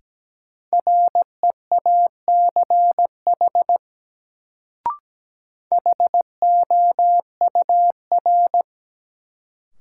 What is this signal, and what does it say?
Morse code, heard as a keyed tone, about 17 wpm, 700 Hz. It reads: REACH HOUR
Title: Sentences from Top 400 Words - New Word Drill - 17wpm